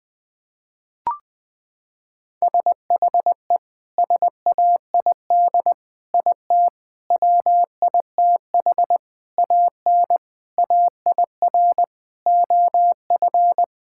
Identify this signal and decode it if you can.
Morse code, heard as a keyed tone, about 20 wpm, 700 Hz. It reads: SHE SAID IT WITH AN AIR OF
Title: Sentences from Top 1600 Words 20wpm